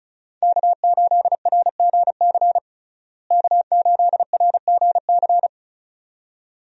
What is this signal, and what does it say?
Morse code, heard as a keyed tone, about 34 wpm, 700 Hz. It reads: K8RGC K8RGC
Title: USA Call Signs - 5 Characters - Repeated 2x - 35wpm